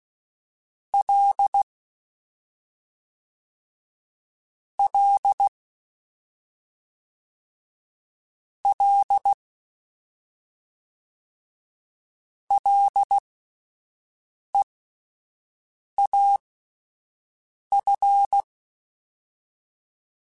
Morse code, heard as a keyed tone, about 16 wpm, spaced as for 5 wpm, 800 Hz. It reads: L L L LEAF